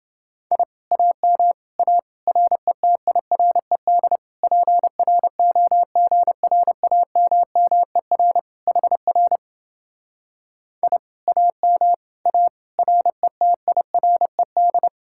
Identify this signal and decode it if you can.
Morse code, heard as a keyed tone, about 28 wpm, 700 Hz. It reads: I AM A RETIRED PROGRAMMER HR I AM A RETIRED